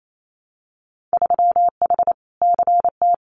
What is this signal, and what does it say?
Morse code, heard as a keyed tone, about 28 wpm, 700 Hz. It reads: 3H CT
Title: Winter Field Day Exchanges Encore 28wpm